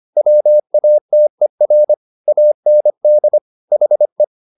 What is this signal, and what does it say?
Morse code, heard as a keyed tone, about 25 wpm, 600 Hz. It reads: WATER AND HE
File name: the old man and the sea part 1 25wpm